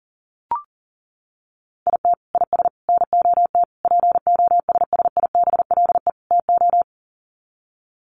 Morse code, heard as a keyed tone, about 40 wpm, 700 Hz. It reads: IT IS NOT POSSIBLE TO
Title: Sets of 5 Words Encore2 40wpm